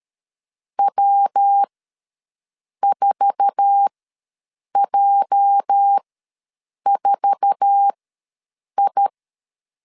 Morse code, heard as a keyed tone, about 13 wpm, spaced as for 7 wpm, 800 Hz. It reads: W4J4I
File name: LEZIONI / 8 YJ45 / 8 YJ4 ws 12wpm 3249